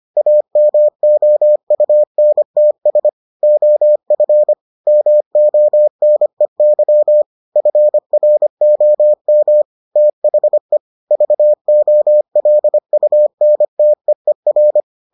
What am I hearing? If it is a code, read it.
Morse code, heard as a keyed tone, about 25 wpm, 600 Hz. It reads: AMOUNTS OF MONEY FROM THE VOLUNTEER